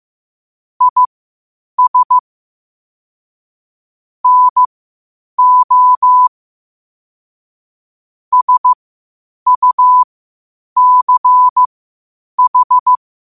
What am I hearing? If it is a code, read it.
Morse code, heard as a keyed tone, about 15 wpm, spaced as for 8 wpm, 1000 Hz. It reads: IS NO SUCH
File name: qotdfiles / 8wpm